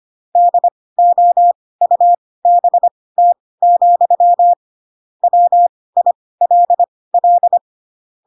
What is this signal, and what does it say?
Morse code, heard as a keyed tone, about 25 wpm, spaced as for 18 wpm, 700 Hz. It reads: DOUBT, WILL
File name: Pinocchio ch3 12 25 ik1hge